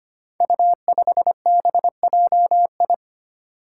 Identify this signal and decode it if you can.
Morse code, heard as a keyed tone, about 25 wpm, 700 Hz. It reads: U5BJI
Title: Five Challenging Letter-Number Combinations 25wpm